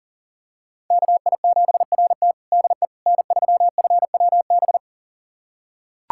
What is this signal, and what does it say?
Morse code, heard as a keyed tone, about 37 wpm, 700 Hz. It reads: KI7RT DE N3FWB
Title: USA Callsign DE Callsign - 5 Characters - 40wpm